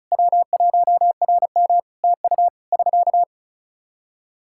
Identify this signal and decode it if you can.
Morse code, heard as a keyed tone, about 36 wpm, 700 Hz. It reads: W1RM TU <SK>
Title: Practice QSO - Exchange 4 - Ending - 35wpm